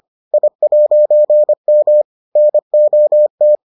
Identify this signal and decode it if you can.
Morse code, heard as a keyed tone, about 25 wpm, 600 Hz. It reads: I'M NOT